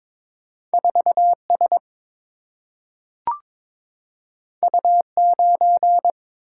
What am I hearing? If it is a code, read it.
Morse code, heard as a keyed tone, about 22 wpm, 700 Hz. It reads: 4S U9
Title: Single-Letter Single-Number 22wpm